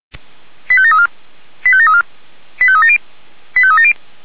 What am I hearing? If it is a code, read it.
Five-tone selective call (ZVEI), heard as quick runs of five tones, tones of about 70 ms. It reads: 85734 85734 85370 85370